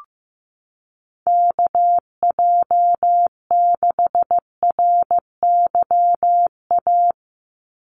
Morse code, heard as a keyed tone, about 15 wpm, 700 Hz. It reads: KJ6RYA
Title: USA Call Signs 15wpm